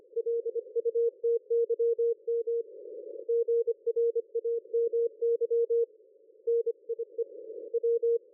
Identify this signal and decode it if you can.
Morse code, heard as a keyed tone, about 25 wpm, 460 Hz. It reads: LUTYM GRAMY NIE W
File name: dlazaawansowanych / nagraniazpasma / ExpertSDR2 line out 23.01.2020 14.12.00